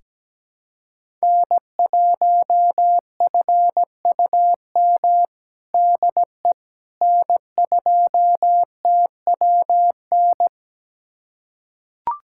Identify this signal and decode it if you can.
Morse code, heard as a keyed tone, about 17 wpm, 700 Hz. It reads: N1FUM DE N2TWN
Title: USA Callsign DE Callsign - 5 Characters - 17wpm